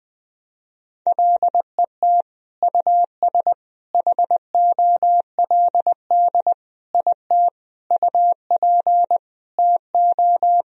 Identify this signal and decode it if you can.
Morse code, heard as a keyed tone, about 21 wpm, 700 Hz. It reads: LET US HOLD IT UP TO
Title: Sentences from Top 1000 Words - Review Part 2 - 20wpm